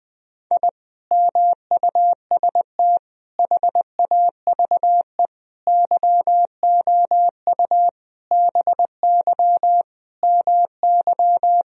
Morse code, heard as a keyed tone, about 20 wpm, 700 Hz. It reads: I MUST HAVE YOU BY MY
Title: Sentences from Top 1000 Words - Review Part 1 - 20wpm